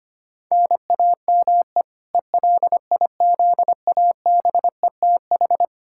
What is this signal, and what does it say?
Morse code, heard as a keyed tone, about 25 wpm, 700 Hz. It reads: NAME ELIZABETH
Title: K1USN SST QSO Practice 25wpm